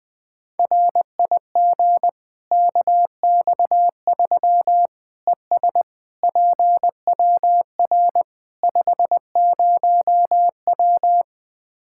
Morse code, heard as a keyed tone, about 20 wpm, 700 Hz. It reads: RIG KX3 ES PWR 50W
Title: Practice QSO - Exchange 2 - Info - 20wpm